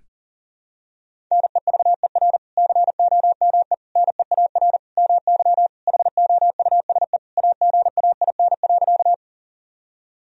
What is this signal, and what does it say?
Morse code, heard as a keyed tone, about 37 wpm, 700 Hz. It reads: NEVER COME NEAR MY HOUSE AGAIN.
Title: Sentences from Top 1000 Words - Review Part 1 - 40wpm